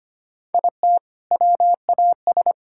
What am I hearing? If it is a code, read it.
Morse code, heard as a keyed tone, about 25 wpm, 700 Hz. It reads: IT WAS
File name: practice / 25wpm / Sentences with 4-Word Phrases - 25wpm